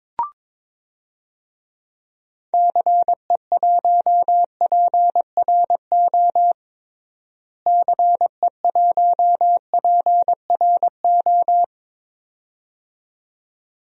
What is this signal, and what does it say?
Morse code, heard as a keyed tone, about 22 wpm, 700 Hz. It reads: CE1PRO CE1PRO
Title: DXCC-100 Call Signs - 6 Characters - Repeated 2x - 22wpm